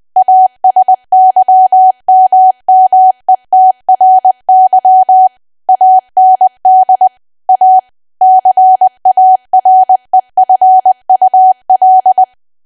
Morse code, heard as a keyed tone, about 20 wpm, 750 Hz. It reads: ASYMMETRY AND A CAREFUL
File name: cwa / W1AW-Level3 / 20WPM